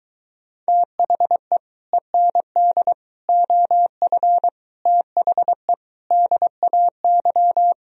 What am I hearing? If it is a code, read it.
Morse code, heard as a keyed tone, about 22 wpm, 700 Hz. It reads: THE END OF THE DAY